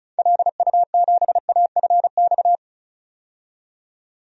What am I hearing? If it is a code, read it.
Morse code, heard as a keyed tone, about 34 wpm, 700 Hz. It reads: LU7AFX
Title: DXCC-100 Call Signs - 6 Characters - 35wpm